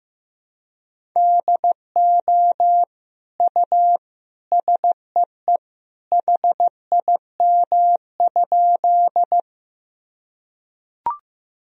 Morse code, heard as a keyed tone, about 15 wpm, 700 Hz. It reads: DO U SEE HIM?
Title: Sentences from Top 100 Words and Abbreviations 15wpm